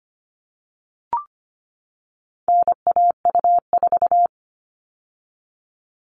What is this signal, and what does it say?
Morse code, heard as a keyed tone, about 25 wpm, 700 Hz. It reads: NAU4
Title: All Cut Numbers - 4-Digit 25wpm